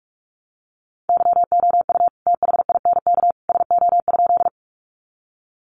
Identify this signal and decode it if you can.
Morse code, heard as a keyed tone, about 44 wpm, 700 Hz. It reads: YOU THINK SO?
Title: Sentences from Top 200 Words 45wpm